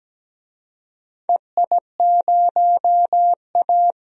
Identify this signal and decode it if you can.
Morse code, heard as a keyed tone, about 17 wpm, 700 Hz. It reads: EI0A
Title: DXCC-100 Call Signs - 4 Characters - 17wpm